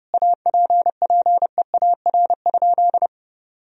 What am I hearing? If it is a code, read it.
Morse code, heard as a keyed tone, about 30 wpm, 700 Hz. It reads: APPEAR?
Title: Sentences from Top 1000 Words - Review Part 2 - 30wpm